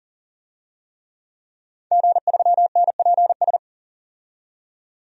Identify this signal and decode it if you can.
Morse code, heard as a keyed tone, about 40 wpm, 700 Hz. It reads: G3NPS
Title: DXCC-100 Call Signs - 5 Characters - 40wpm